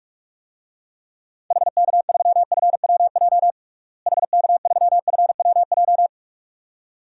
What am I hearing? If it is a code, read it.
Morse code, heard as a keyed tone, about 41 wpm, 700 Hz. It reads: HK3FWJ HK3FWJ
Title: DXCC-100 Call Signs - 6 Characters - Repeated 2x - 45wpm